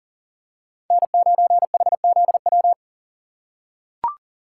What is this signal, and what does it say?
Morse code, heard as a keyed tone, about 40 wpm, 700 Hz. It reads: N9HZW
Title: USA Call Signs - 5 Characters - 4x Character Spacing - 40wpm